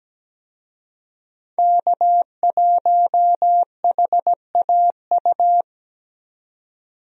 Morse code, heard as a keyed tone, about 17 wpm, 700 Hz. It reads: K1HAU
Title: USA Call Signs - 5 Characters - 17wpm